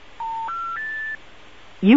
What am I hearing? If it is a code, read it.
Special information tone, with speech.
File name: attntav